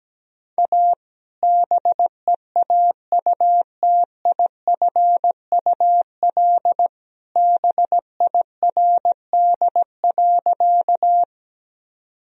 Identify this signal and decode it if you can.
Morse code, heard as a keyed tone, about 17 wpm, 700 Hz. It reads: A BEAUTIFUL BIRD.
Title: Sentences from Top 1000 Words - Review Part 2 - 17wpm